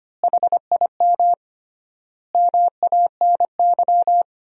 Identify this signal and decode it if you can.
Morse code, heard as a keyed tone, about 25 wpm, 700 Hz. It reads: HIM MANY
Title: 3x Word-Spacing - Sentences from Top 200 Words - 25wpm